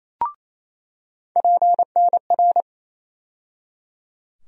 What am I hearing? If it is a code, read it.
Morse code, heard as a keyed tone, about 28 wpm, 700 Hz. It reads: PNR
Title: USA Call Sign Postfixes - 3 Letters 28wpm